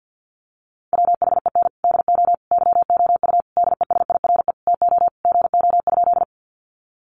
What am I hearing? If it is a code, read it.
Morse code, heard as a keyed tone, about 51 wpm, 700 Hz. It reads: WHEN DO YOU DESIRE TO GO?
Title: Sentences from Top 1100 Words 50wpm